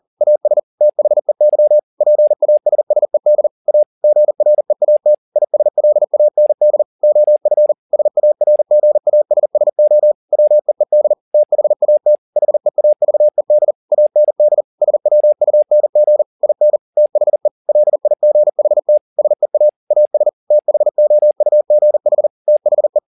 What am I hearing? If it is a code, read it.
Morse code, heard as a keyed tone, about 40 wpm, 600 Hz. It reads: AS THEY PASSED A GREAT ISLAND OF SARGASSO WEED THAT HEAVED AND SWUNG IN THE LIGHT SEA AS THOUGH THE
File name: the old man and the sea 2 40wpm